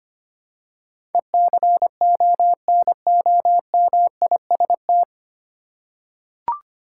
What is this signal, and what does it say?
Morse code, heard as a keyed tone, about 25 wpm, 700 Hz. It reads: ECONOMIST
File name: Nine Letter Words - 25wpm